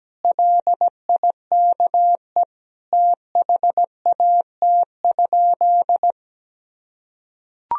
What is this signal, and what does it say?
Morse code, heard as a keyed tone, about 17 wpm, 700 Hz. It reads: LIKE THAT?
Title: Sentences from Top 1000 Words - Review Part 4 - 17wpm